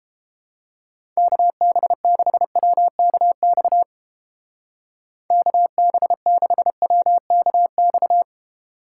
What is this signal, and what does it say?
Morse code, heard as a keyed tone, about 31 wpm, 700 Hz. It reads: KB6WKX KB6WKX